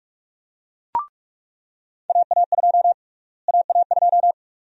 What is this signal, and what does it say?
Morse code, heard as a keyed tone, about 44 wpm, 700 Hz. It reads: AA2 AA2
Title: USA Call Sign Prefixes - Repeated 2x - 45wpm